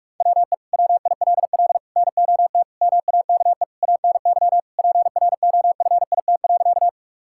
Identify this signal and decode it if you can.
Morse code, heard as a keyed tone, about 46 wpm, 700 Hz. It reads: WE WILL NOT MAKE ANY PROFIT.
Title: Sentences from Top 1000 Words - Review Part 5 - 45wpm